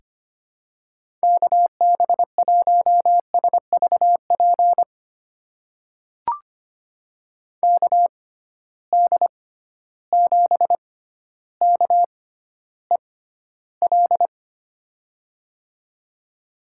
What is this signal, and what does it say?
Morse code, heard as a keyed tone, about 25 wpm, spaced as for 13 wpm, 700 Hz. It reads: KB1SVP KD7KEL
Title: USA Call Signs - 6x Character Spacing - 25wpm